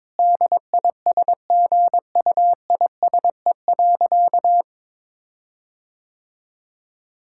Morse code, heard as a keyed tone, about 22 wpm, 700 Hz. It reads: DISGUISE.
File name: Difficult Sentences with Hints - 22wpm